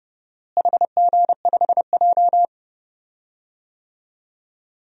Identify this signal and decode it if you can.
Morse code, heard as a keyed tone, about 29 wpm, 700 Hz. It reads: HG5J
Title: DXCC-100 Call Signs - 4 Characters - Repeated 3x - 30wpm